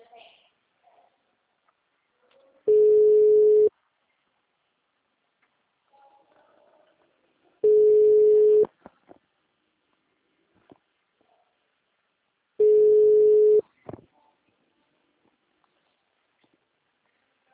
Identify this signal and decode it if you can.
Ringback tone (Europe, 425 Hz), with speech.